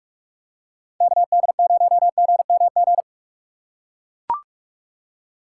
Morse code, heard as a keyed tone, about 44 wpm, 700 Hz. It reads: KD0GMG
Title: USA Call Signs - 6 Characters - 2x Character Spacing - 45wpm